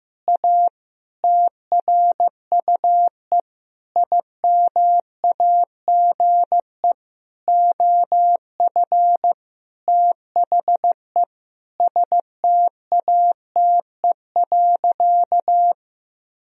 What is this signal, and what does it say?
Morse code, heard as a keyed tone, about 15 wpm, 700 Hz. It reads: A TRUE IMAGE OF THE STATE.
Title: Sentences from Top 1000 Words - Review Part 4 - 15wpm